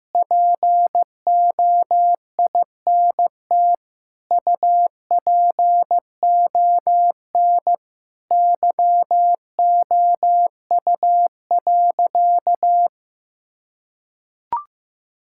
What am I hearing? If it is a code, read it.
Morse code, heard as a keyed tone, about 16 wpm, 700 Hz. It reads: POINT UPON YOU.
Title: Sentences from Top 1600 Words 15wpm